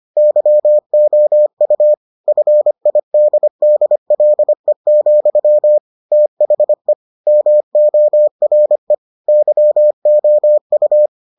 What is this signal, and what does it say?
Morse code, heard as a keyed tone, about 25 wpm, 600 Hz. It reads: YOU FIDDLE, THE MORE YOU